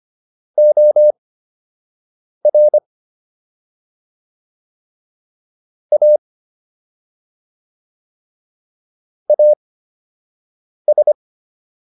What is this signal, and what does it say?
Morse code, heard as a keyed tone, about 25 wpm, spaced as for 6 wpm, 600 Hz. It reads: OR A AS